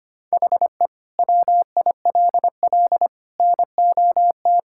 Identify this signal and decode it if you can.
Morse code, heard as a keyed tone, about 25 wpm, 700 Hz. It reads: HE WILL NOT
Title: Sentences from Top 1000 Words - Review Part 6 - 25wpm